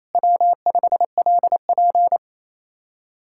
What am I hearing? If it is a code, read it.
Morse code, heard as a keyed tone, about 36 wpm, 700 Hz. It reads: W5LP